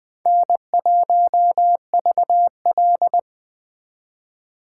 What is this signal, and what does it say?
Morse code, heard as a keyed tone, about 20 wpm, 700 Hz. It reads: N1VL
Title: Speed Racing - USA 1x2 Call Signs 17wpm